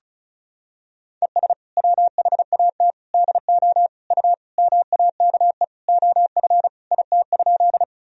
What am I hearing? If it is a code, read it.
Morse code, heard as a keyed tone, about 36 wpm, 700 Hz. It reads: ES WHAT DO U MAKE OF IT?
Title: Sentences from Top 100 Words and Abbreviations 35wpm